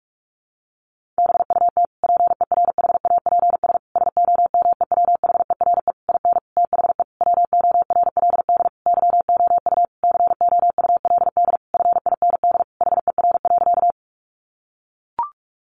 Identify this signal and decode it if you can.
Morse code, heard as a keyed tone, about 44 wpm, 700 Hz. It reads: BUT PERHAPS SOMEWHERE IN THE WORLD YOU COULD FIND HER.
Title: Sentences from Top 1400 Words 45wpm